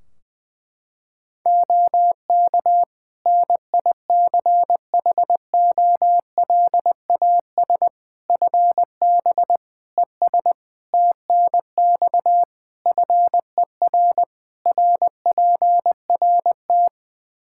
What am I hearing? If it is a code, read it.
Morse code, heard as a keyed tone, about 20 wpm, 700 Hz. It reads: OK NICHOLAS FB ES TNX FER RPRT
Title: Practice QSO - Exchange 2 - Info - 20wpm